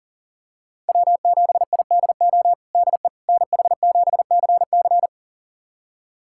Morse code, heard as a keyed tone, about 40 wpm, 700 Hz. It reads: W7IDO DE NH7CC
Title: USA Callsign DE Callsign - 5 Characters - 40wpm